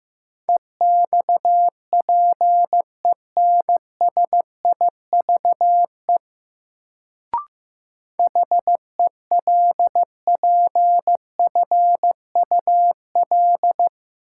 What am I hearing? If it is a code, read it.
Morse code, heard as a keyed tone, about 15 wpm, 700 Hz. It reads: EXPENSIVE HELPFUL
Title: Sentences from Top 1000 Words - Review Part 6 - New Word Drill - 15wpm